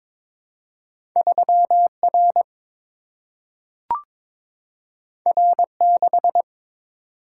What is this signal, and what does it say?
Morse code, heard as a keyed tone, about 22 wpm, 700 Hz. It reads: 3 R R 6
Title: Single-Letter Single-Number 22wpm